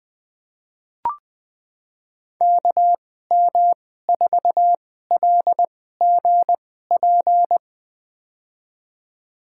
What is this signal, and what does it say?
Morse code, heard as a keyed tone, about 20 wpm, spaced as for 14 wpm, 700 Hz. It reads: KM4LGP
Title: USA Call Signs - 6 Characters - 2x Character Spacing - 20wpm